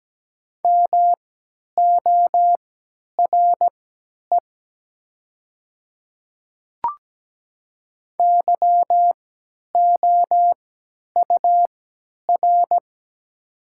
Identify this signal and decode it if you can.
Morse code, heard as a keyed tone, about 17 wpm, spaced as for 10 wpm, 700 Hz. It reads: MORE YOUR
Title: 3x Character Spacing - Top 100 Words - 17wpm